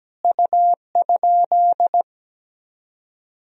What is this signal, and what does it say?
Morse code, heard as a keyed tone, about 17 wpm, 700 Hz. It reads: U?
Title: Sentences from Top 100 Words and Abbreviations 17wpm